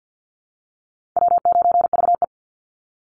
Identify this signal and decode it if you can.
Morse code, heard as a keyed tone, about 49 wpm, 700 Hz. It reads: W9VE
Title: Easy Call Signs Repeated 3x - 50wpm